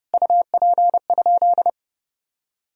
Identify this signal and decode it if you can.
Morse code, heard as a keyed tone, about 30 wpm, 700 Hz. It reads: UP?
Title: Sentences from Top 1000 Words - Review Part 1 - 30wpm